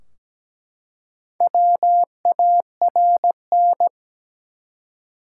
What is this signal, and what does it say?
Morse code, heard as a keyed tone, about 17 wpm, 700 Hz. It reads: WARN
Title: Four Letter Words - 17wpm